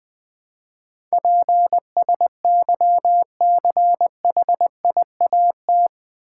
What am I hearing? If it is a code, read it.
Morse code, heard as a keyed tone, about 20 wpm, 700 Hz. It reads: PSYCHIAT
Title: Build A Word - 4 Letters At A Time - 11-Letter Words - 20wpm